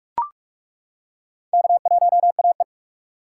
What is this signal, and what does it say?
Morse code, heard as a keyed tone, about 42 wpm, 700 Hz. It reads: K1AE
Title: K1USN SST QSO Practice - 45wpm